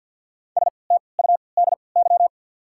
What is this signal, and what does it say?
Morse code, heard as a keyed tone, about 53 wpm, 690 Hz. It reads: S T U D Y
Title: 3x Character Spacing - Top 300 Words - 50wpm